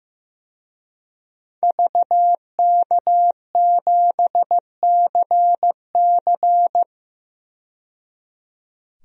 Morse code, heard as a keyed tone, about 15 wpm, 700 Hz. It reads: VK7CC
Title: DXCC-100 Call Signs - 5 Characters - 15wpm